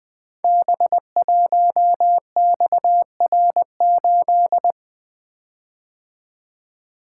Morse code, heard as a keyed tone, about 20 wpm, 700 Hz. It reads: B1XR8
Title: Code Groups - 5 Letter:Numbers - 20wpm